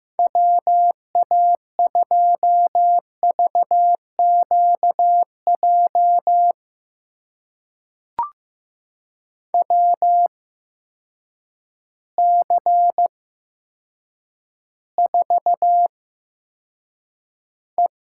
Morse code, heard as a keyed tone, about 15 wpm, spaced as for 8 wpm, 700 Hz. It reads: WA2VQJ WC4E
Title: USA Call Signs - 8x Character Spacing - 15wpm